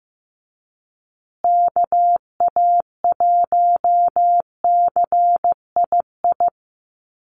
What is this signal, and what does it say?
Morse code, heard as a keyed tone, about 15 wpm, 700 Hz. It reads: KA1CII
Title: USA Call Signs Repeated 3x 15wpm